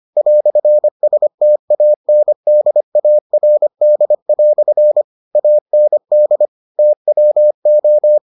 Morse code, heard as a keyed tone, about 25 wpm, 600 Hz. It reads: "STANDARD" AND TWO